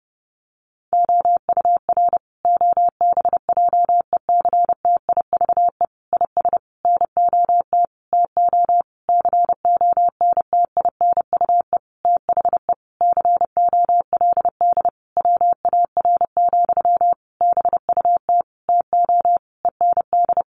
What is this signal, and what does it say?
Morse code, heard as a keyed tone, about 30 wpm, 700 Hz. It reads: OUR OBJECTIVE IS NOT TO CONTINUE THE COLD WAR, BUT TO END
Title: Sentences from Top 1200 Words 30wpm